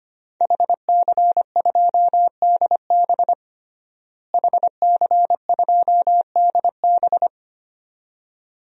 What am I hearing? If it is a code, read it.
Morse code, heard as a keyed tone, about 25 wpm, 700 Hz. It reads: HC2DB HC2DB
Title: DXCC-100 Call Signs - 5 Characters - Repeated 2x - 25wpm